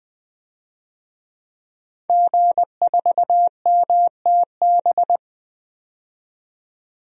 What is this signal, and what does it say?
Morse code, heard as a keyed tone, about 21 wpm, 700 Hz. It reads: G4MTB
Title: DXCC-100 Call Signs - 5 Characters - 20wpm